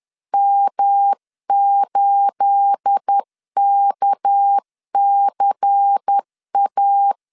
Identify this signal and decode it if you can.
Morse code, heard as a keyed tone, about 10 wpm, 800 Hz. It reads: M8KCA
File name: LEZIONI / 9 28 10wpm 2842